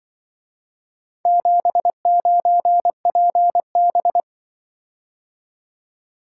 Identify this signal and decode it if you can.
Morse code, heard as a keyed tone, about 24 wpm, 700 Hz. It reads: 79PB
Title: Speed-Racing - Four Letter-Number Combinations - 20wpm